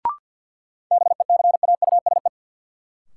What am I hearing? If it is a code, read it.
Morse code, heard as a keyed tone, about 47 wpm, 700 Hz. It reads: BECAUSE
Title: Sets of 2 Words - New Word Drill - 50wpm